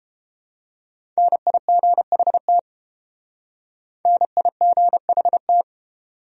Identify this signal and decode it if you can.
Morse code, heard as a keyed tone, about 29 wpm, 700 Hz. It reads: NIGHT NIGHT